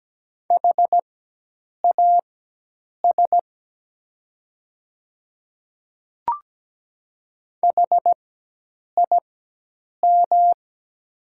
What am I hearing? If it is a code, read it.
Morse code, heard as a keyed tone, about 17 wpm, spaced as for 8 wpm, 690 Hz. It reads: HAS HIM
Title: 4x Character Spacing - Top 100 Words - 17wpm